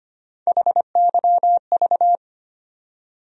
Morse code, heard as a keyed tone, about 25 wpm, 700 Hz. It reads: HYV
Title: USA Call Sign Postfixes - 3 Letters 25wpm